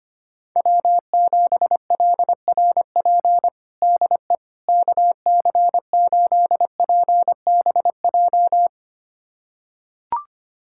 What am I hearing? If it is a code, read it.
Morse code, heard as a keyed tone, about 25 wpm, 700 Hz. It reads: W7LRP DE KC8PBJ
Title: USA Callsign DE Callsign 25wpm